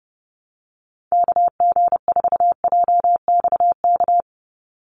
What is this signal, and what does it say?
Morse code, heard as a keyed tone, about 30 wpm, 700 Hz. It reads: KG4JXK
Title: USA Call Signs 30wpm